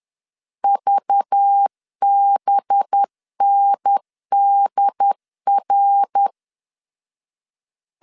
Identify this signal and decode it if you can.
Morse code, heard as a keyed tone, about 10 wpm, 800 Hz. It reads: VBNDR